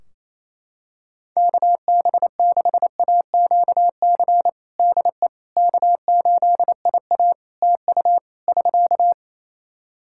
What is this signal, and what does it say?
Morse code, heard as a keyed tone, about 28 wpm, 700 Hz. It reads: KB6AQC DE K8IA TU <SK>
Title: K1USN SST QSO Practice 28wpm